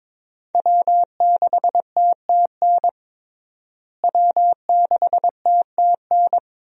Morse code, heard as a keyed tone, about 22 wpm, 700 Hz. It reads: W6TTN W6TTN
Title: USA Call Signs - 5 Characters - Repeated 2x - 22wpm